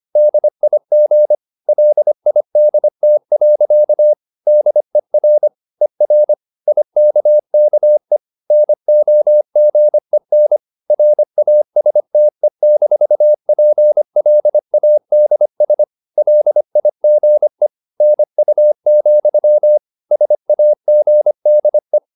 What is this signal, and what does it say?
Morse code, heard as a keyed tone, about 25 wpm, 600 Hz. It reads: DIG LIDT. DER ER IKKE NOGEN RASTE-PLADS LIGE NU, SAGDE
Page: Ferie 25 wpm